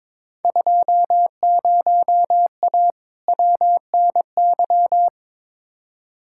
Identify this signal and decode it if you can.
Morse code, heard as a keyed tone, about 22 wpm, 700 Hz. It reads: 20A WNY
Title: Field Day Exchanges Encore - 22wpm